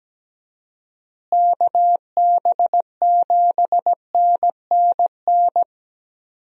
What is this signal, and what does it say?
Morse code, heard as a keyed tone, about 17 wpm, 700 Hz. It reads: KB7NNN
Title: USA Call Signs - 6 Characters - Repeated 2x - 17wpm